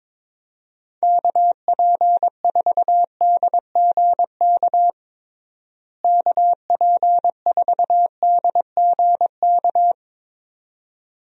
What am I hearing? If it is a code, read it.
Morse code, heard as a keyed tone, about 22 wpm, 700 Hz. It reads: KP4DGK KP4DGK
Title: USA Call Signs - 6 Characters - Repeated 2x - 22wpm